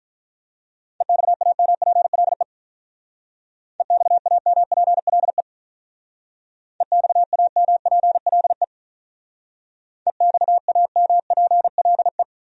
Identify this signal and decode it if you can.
Morse code, heard as a keyed tone, about 43 wpm, 700 Hz. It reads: EXAMPLE EXAMPLE EXAMPLE EXAMPLE
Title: Speed-Racing Top 300 Words 35wpm